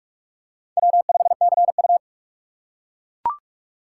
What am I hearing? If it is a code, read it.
Morse code, heard as a keyed tone, about 42 wpm, 700 Hz. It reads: W5CU
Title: USA Call Signs - 4 Characters - 6x Character Spacing - 45wpm